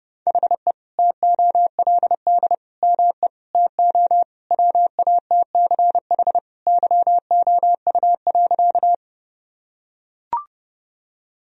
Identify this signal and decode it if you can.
Morse code, heard as a keyed tone, about 30 wpm, 700 Hz. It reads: HE TOLD ME TO WATCH YOU.
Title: Sentences from Top 1000 Words - Review Part 1 - 30wpm